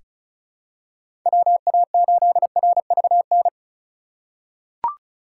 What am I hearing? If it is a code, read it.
Morse code, heard as a keyed tone, about 34 wpm, 700 Hz. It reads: WA8RVN
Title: USA Call Signs - 6 Characters - 2x Character Spacing - 35wpm